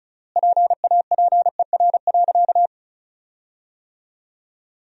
Morse code, heard as a keyed tone, about 34 wpm, 700 Hz. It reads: PAPER.
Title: 3x Word-Spacing - Sentences from Top 500 Words - 35wpm